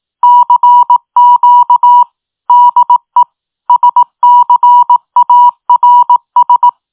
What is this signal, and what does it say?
Morse code, heard as a keyed tone, about 18 wpm, 970 Hz. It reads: CQ DE SCARS